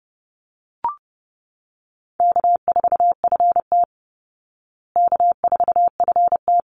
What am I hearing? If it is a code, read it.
Morse code, heard as a keyed tone, about 30 wpm, 700 Hz. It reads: K4FT K4FT
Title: Call Signs Repeated 2x - 30wpm